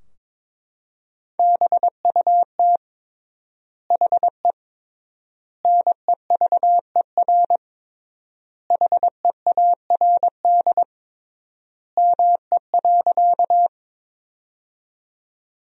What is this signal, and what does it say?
Morse code, heard as a keyed tone, about 22 wpm, 700 Hz. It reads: BUT HE NEVER HEARD ME.
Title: 3x Word-Spacing - Sentences from Top 400 Words - 22wpm